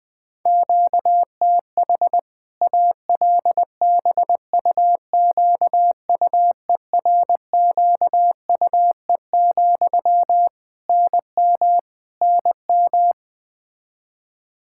Morse code, heard as a keyed tone, about 20 wpm, 700 Hz. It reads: QTH ALBUQUERQUE, NM NM
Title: Practice QSO - Exchange 1 - Beginning - 20wpm